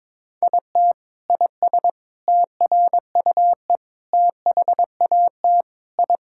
Morse code, heard as a keyed tone, about 22 wpm, 700 Hz. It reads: IT IS TRUE THAT I
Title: Sentences from Top 1000 Words - Review Part 6 - 22wpm